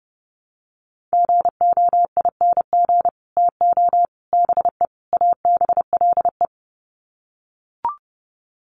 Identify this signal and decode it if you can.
Morse code, heard as a keyed tone, about 30 wpm, 700 Hz. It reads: GOING TO BE ABLE
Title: Sets of 4 Words 30wpm